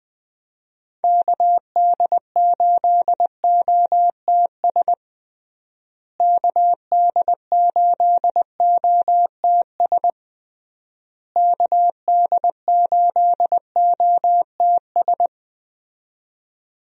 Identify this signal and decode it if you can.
Morse code, heard as a keyed tone, about 20 wpm, 700 Hz. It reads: KD8OTS KD8OTS KD8OTS
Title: USA Call Signs - 6 Characters - Repeated 3x - 20wpm